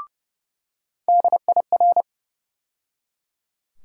Morse code, heard as a keyed tone, about 30 wpm, 700 Hz. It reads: DIR
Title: USA Call Sign Postfixes - 3 Letters 30wpm